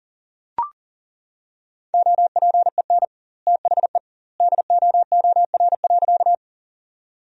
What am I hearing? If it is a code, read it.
Morse code, heard as a keyed tone, about 40 wpm, 700 Hz. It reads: OPEN THE DOOR.
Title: 2x Word-Spacing - Sentences from Top 300 Words - 40wpm